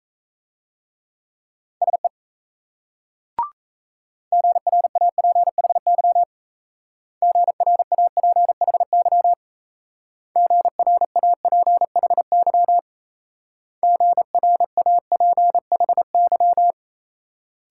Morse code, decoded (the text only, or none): SE GRAPHY GRAPHY GRAPHY GRAPHY